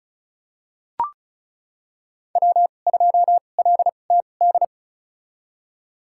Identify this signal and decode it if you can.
Morse code, heard as a keyed tone, about 36 wpm, spaced as for 25 wpm, 700 Hz. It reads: W2LTD
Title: USA Call Signs - 2x Character Spacing - 35wpm